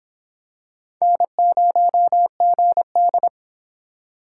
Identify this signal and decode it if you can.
Morse code, heard as a keyed tone, about 25 wpm, 700 Hz. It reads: N0GD